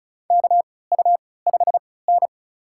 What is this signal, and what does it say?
Morse code, heard as a keyed tone, about 36 wpm, spaced as for 20 wpm, 700 Hz. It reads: KU5N
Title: USA Call Signs - 4 Characters - 3x Character Spacing - 35wpm